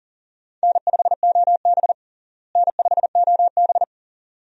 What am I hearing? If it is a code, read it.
Morse code, heard as a keyed tone, about 37 wpm, 700 Hz. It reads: N5OB N5OB
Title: USA Call Signs - 4 Characters - Repeated 2x - 40wpm